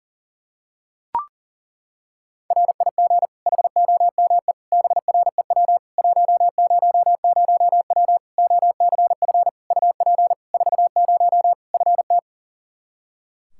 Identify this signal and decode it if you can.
Morse code, heard as a keyed tone, about 37 wpm, 700 Hz. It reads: RIG HOME BREW 100W OCF UP 40 FT
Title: K1USN SST QSO Practice 40wpm